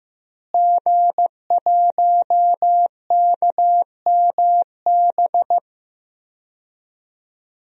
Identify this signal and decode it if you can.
Morse code, heard as a keyed tone, about 15 wpm, 700 Hz. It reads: G1KMB
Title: DXCC-100 Call Signs - 5 Characters - Repeated 2x - 15wpm